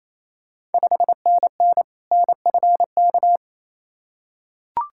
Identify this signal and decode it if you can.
Morse code, heard as a keyed tone, about 28 wpm, 700 Hz. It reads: 5NN NFK
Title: Ontario QSO Party Exchange 28wpm